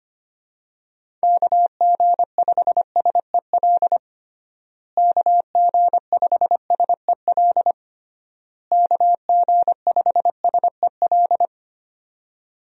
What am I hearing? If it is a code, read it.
Morse code, heard as a keyed tone, about 25 wpm, 700 Hz. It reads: KG5SEL KG5SEL KG5SEL
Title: USA Call Signs - 6 Characters - Repeated 3x - 25wpm